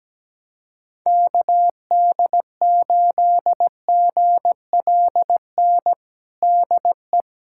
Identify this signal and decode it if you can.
Morse code, heard as a keyed tone, about 17 wpm, 700 Hz. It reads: KD8GLN DE
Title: USA Callsign DE Callsign 17wpm